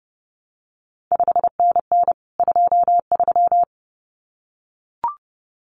Morse code, heard as a keyed tone, about 30 wpm, 700 Hz. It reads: 5NN 23
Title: CQ WW DX Contest Exchange Encore - 30wpm